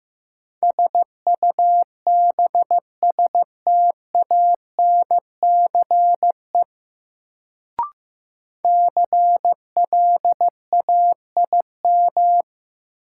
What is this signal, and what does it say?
Morse code, heard as a keyed tone, about 15 wpm, 700 Hz. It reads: SUBSTANCE CLAIM
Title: Sentences from Top 1500 Words - New Word Drill Alt - 15wpm